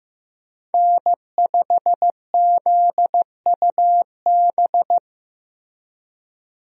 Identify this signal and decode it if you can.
Morse code, heard as a keyed tone, about 15 wpm, 700 Hz. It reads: N5ZUB
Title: USA Call Signs - 5 Characters - 15wpm